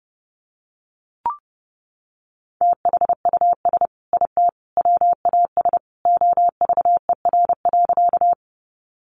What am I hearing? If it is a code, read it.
Morse code, heard as a keyed tone, about 30 wpm, 700 Hz. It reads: THUS IT WAS OVER.
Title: Sentences from Top 800 Words 30wpm